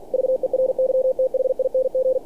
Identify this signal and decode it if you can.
Morse code, heard as a keyed tone, about 48 wpm, 520 Hz. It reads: VERYTHING
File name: archived / CRSnet